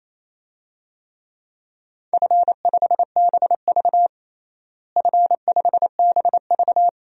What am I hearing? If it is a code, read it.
Morse code, heard as a keyed tone, about 28 wpm, 700 Hz. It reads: F5BV F5BV
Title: DXCC-100 Call Signs - 4 Characters - Repeated 2x - 28wpm